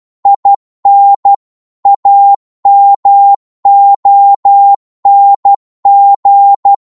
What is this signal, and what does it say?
Morse code, heard as a keyed tone, about 12 wpm, 800 Hz. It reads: IN AMONG